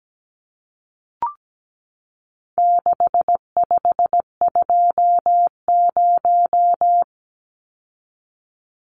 Morse code, heard as a keyed tone, about 17 wpm, 700 Hz. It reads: 6520
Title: Numbers - 4-Digit 17wpm